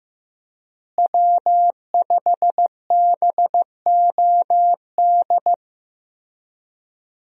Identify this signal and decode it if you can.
Morse code, heard as a keyed tone, about 15 wpm, 700 Hz. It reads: W5BOD
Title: USA Call Signs - 5 Characters - Repeated 3x - 15wpm